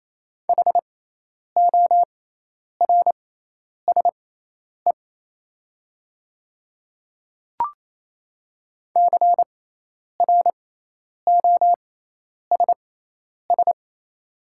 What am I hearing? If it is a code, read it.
Morse code, heard as a keyed tone, about 27 wpm, spaced as for 10 wpm, 700 Hz. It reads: HORSE CROSS
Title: 6x Character Spacing - Top 300 Words - 28wpm